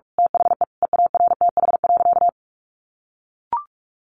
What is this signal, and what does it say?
Morse code, heard as a keyed tone, about 44 wpm, 700 Hz. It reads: THE EARTH.
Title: A Sound Of Thunder 45wpm